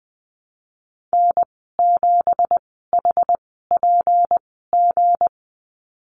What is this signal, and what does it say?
Morse code, heard as a keyed tone, about 20 wpm, spaced as for 15 wpm, 700 Hz. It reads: N7HPG
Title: 167 Lesson 27g - Number 7 - Call Signs 20wpm